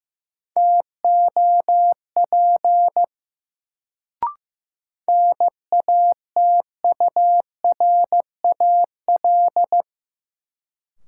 Morse code, heard as a keyed tone, about 15 wpm, 700 Hz. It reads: TOP NATURAL
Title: Sentences from Top 700 Words - New Word Drill - 15wpm